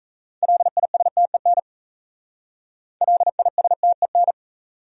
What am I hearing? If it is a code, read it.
Morse code, heard as a keyed tone, about 37 wpm, 700 Hz. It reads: LISTEN LISTEN